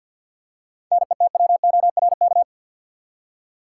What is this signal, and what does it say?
Morse code, heard as a keyed tone, about 51 wpm, 700 Hz. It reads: NETWORK
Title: Seven Letter Words - 50wpm